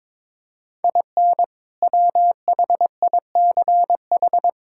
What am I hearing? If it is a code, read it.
Morse code, heard as a keyed tone, about 22 wpm, 700 Hz. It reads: IN WHICH
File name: practice / 22wpm / Sentences with 2-Word Phrases - 22wpm